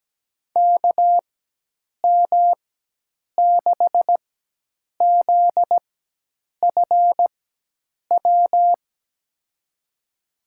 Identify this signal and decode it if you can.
Morse code, heard as a keyed tone, about 17 wpm, spaced as for 8 wpm, 700 Hz. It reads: KM6ZFW
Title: USA Call Signs - 6 Characters - 4x Character Spacing - 17wpm